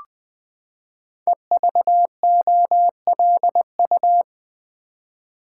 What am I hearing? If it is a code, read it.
Morse code, heard as a keyed tone, about 20 wpm, 700 Hz. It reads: EVOLU
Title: Build A Word - 5 Letters At A Time - 12-Letter Words - 20wpm